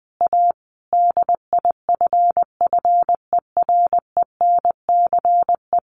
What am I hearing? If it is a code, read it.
Morse code, heard as a keyed tone, about 20 wpm, 700 Hz. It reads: A DIFFERENCE
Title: Sentences from Top 1400 Words 20wpm